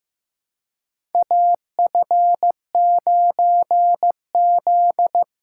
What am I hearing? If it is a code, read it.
Morse code, heard as a keyed tone, about 15 wpm, 700 Hz. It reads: AF9Z
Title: USA Call Signs - 4 Characters - 2x Character Spacing - 15wpm